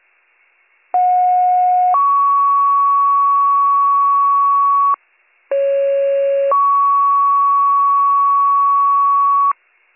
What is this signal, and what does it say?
Two-tone page, heard as a one-second tone, then a three-second tone, about 730 then 1120 Hz, about 550 then 1120 Hz. It is heard from 0.9 to 9.5 s.